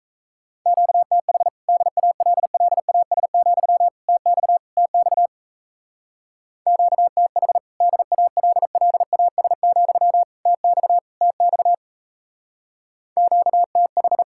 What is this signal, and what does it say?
Morse code, heard as a keyed tone, about 32 wpm, 700 Hz. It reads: QTH DALLAS, TX TX QTH DALLAS, TX TX QTH
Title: Speed-Racing Common QSO Phrases - 28wpm